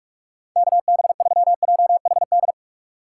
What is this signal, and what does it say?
Morse code, heard as a keyed tone, about 46 wpm, 700 Hz. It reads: KB3JHD
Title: USA Call Signs - 6 Characters - 4x Character Spacing - 45wpm